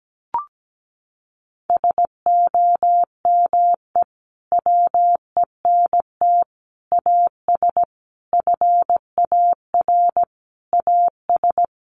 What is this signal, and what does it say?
Morse code, heard as a keyed tone, about 17 wpm, 700 Hz. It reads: SOME WENT AS FAR AS
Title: Sentences from Top 1100 Words 17wpm